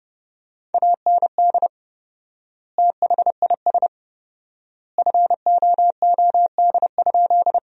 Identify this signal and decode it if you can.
Morse code, heard as a keyed tone, about 32 wpm, 700 Hz. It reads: AND THIS FOOD?
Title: 4x Word-Spacing - Sentences from Top 300 Words - 30wpm